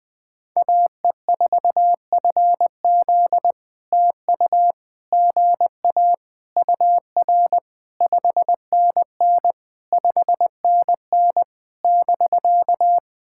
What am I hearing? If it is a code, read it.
Morse code, heard as a keyed tone, about 19 wpm, 700 Hz. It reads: AE4FZ TU GA UR 5NN 5NN <BK>
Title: POTA QSO Practice - 20wpm